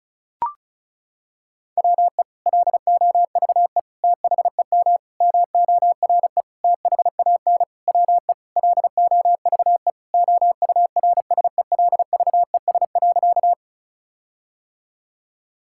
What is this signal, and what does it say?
Morse code, heard as a keyed tone, about 36 wpm, 700 Hz. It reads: WE LOVE THEM MORE THAN WE LOVE OURSELVES.
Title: Sentences from Top 1600 Words 35wpm